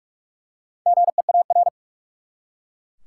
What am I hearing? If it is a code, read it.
Morse code, heard as a keyed tone, about 43 wpm, 700 Hz. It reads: GEAR
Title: Four Letter Words - 45wpm